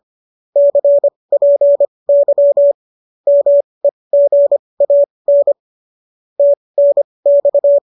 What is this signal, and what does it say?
Morse code, heard as a keyed tone, about 25 wpm, spaced as for 20 wpm, 570 Hz. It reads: CPY MEGAN TNX